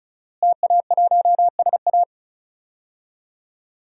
Morse code, heard as a keyed tone, about 36 wpm, 700 Hz. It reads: TA1SA
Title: DXCC-100 Call Signs - 5 Characters - Repeated 2x - 35wpm